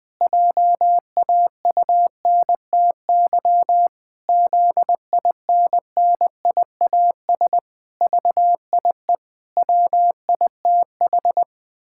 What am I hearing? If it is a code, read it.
Morse code, heard as a keyed tone, about 20 wpm, 700 Hz. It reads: JAUNTY ZINNIAS VIE WITH
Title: Pangram Challenge v2 - 20wpm